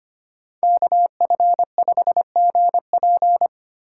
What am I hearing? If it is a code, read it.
Morse code, heard as a keyed tone, about 29 wpm, 700 Hz. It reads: KF5GP
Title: Speed Racing - USA Call Signs 25wpm